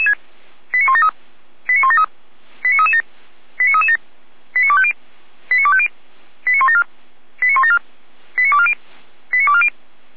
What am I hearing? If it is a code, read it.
Five-tone selective call (ZVEI), heard as quick runs of five tones, tones of about 70 ms. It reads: 89173 89173 89337 89337 89250 89250 89174 89174 89240 89240